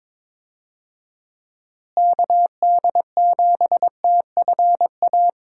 Morse code, heard as a keyed tone, about 22 wpm, 700 Hz. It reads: KD7TFA
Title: USA Call Signs - 6 Characters - Repeated 3x - 22wpm